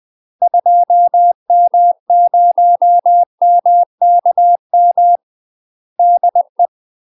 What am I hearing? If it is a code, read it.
Morse code, heard as a keyed tone, about 20 wpm, 700 Hz. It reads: 2M0MKM DE
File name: UK MASTER QSOs TRAINING / 20WPM / UK MASTER QSOs 0014